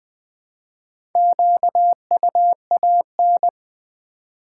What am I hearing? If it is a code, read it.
Morse code, heard as a keyed tone, about 20 wpm, 700 Hz. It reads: QUAN
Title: Build A Word - 4 Letters At A Time - 8-Letter Words - 20wpm